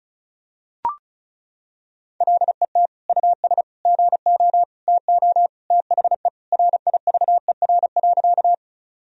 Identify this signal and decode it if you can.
Morse code, heard as a keyed tone, about 36 wpm, 700 Hz. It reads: LET US GO TO THE RIVER.
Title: Sentences from Top 1000 Words - Review Part 1 - 35wpm